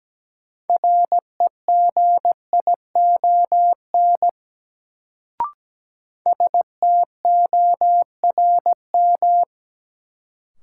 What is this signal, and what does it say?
Morse code, heard as a keyed tone, about 17 wpm, 700 Hz. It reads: REGION STORM
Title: Sentences from Top 1500 Words - New Word Drill Alt - 17wpm